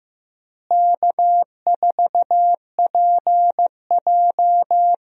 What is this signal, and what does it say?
Morse code, heard as a keyed tone, about 15 wpm, 700 Hz. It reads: K4PJ
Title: USA Call Signs - 4 Characters - Repeated 3x - 15wpm